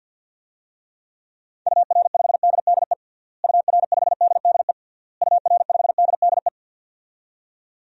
Morse code, heard as a keyed tone, about 52 wpm, 700 Hz. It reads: UR5DDE UR5DDE UR5DDE
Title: DXCC-100 Call Signs - 6 Characters - Repeated 3x - 50wpm